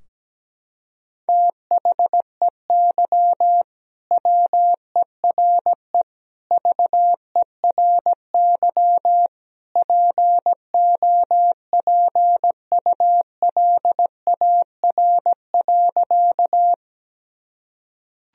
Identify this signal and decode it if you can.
Morse code, heard as a keyed tone, about 17 wpm, 700 Hz. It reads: THEY WERE VERY POPULAR.
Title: Sentences from Top 1000 Words - Review Part 3 - 17wpm